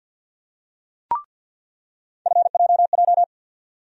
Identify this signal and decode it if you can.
Morse code, heard as a keyed tone, about 48 wpm, 700 Hz. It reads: FJJ
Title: USA Call Sign Postfixes - 3 Letters 50wpm